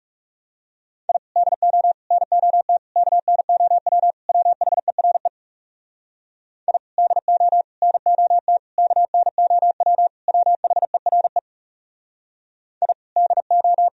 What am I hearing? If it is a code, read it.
Morse code, heard as a keyed tone, about 40 wpm, 700 Hz. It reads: I DO NOT KNOW WHERE I DO NOT KNOW WHERE I DO